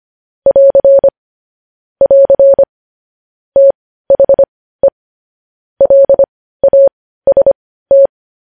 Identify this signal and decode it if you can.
Morse code, heard as a keyed tone, about 25 wpm, spaced as for 15 wpm, 550 Hz. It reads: <AR> <AR> THE LAST
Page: news-headlines-in-morse-code-at-15-wpm